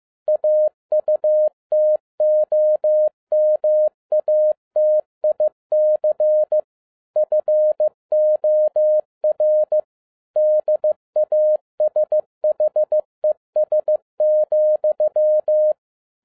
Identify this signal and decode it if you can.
Morse code, heard as a keyed tone, about 15 wpm, 610 Hz. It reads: AUTOMATIC FOR DASHES,